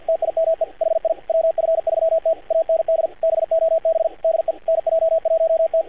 Morse code, heard as a keyed tone, about 45 wpm, 590 Hz, with radio noise. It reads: TIME HI WF3T AND BOB DE NJ1T
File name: archived / CRSnet